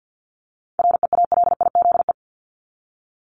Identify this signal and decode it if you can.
Morse code, heard as a keyed tone, about 50 wpm, 700 Hz. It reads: REALIZE
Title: Top 751-1000 Words 50wpm